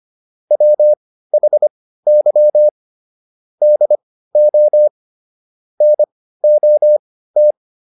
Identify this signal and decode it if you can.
Morse code, heard as a keyed tone, about 25 wpm, spaced as for 15 wpm, 600 Hz. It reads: WHY DO NOT
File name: WD404 15